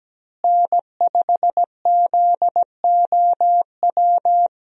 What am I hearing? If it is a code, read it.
Morse code, heard as a keyed tone, about 17 wpm, 700 Hz. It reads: N5ZOW
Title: USA Call Signs - 5 Characters - Repeated 2x - 17wpm